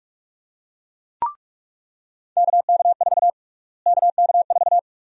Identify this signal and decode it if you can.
Morse code, heard as a keyed tone, about 46 wpm, 690 Hz. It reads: KK4 KK4
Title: USA Call Sign Prefixes - Repeated 2x - 45wpm